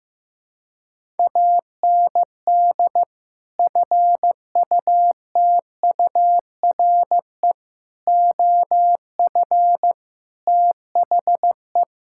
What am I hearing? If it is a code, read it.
Morse code, heard as a keyed tone, about 15 wpm, 700 Hz. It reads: AND FUTURE OF THE
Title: Sentences from Top 1600 Words 15wpm